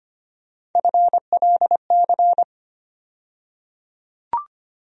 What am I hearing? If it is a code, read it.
Morse code, heard as a keyed tone, about 25 wpm, 700 Hz. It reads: FLC
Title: USA Call Sign Postfixes - 3 Letters 25wpm